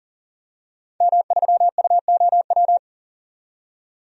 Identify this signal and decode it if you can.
Morse code, heard as a keyed tone, about 40 wpm, 700 Hz. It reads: M3UOW
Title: DXCC-100 Call Signs - 5 Characters - 40wpm